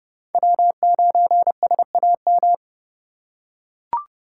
Morse code, heard as a keyed tone, about 30 wpm, 700 Hz. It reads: W9SAM
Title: USA Call Signs - 4x Character Spacing - 30wpm